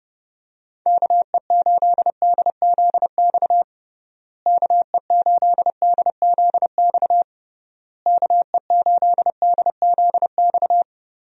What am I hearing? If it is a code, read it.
Morse code, heard as a keyed tone, about 30 wpm, 700 Hz. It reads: KE8DZX KE8DZX KE8DZX
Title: USA Call Signs - 6 Characters - Repeated 3x - 30wpm